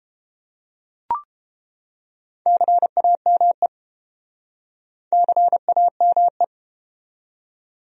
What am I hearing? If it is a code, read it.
Morse code, heard as a keyed tone, about 29 wpm, 700 Hz. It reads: CAME CAME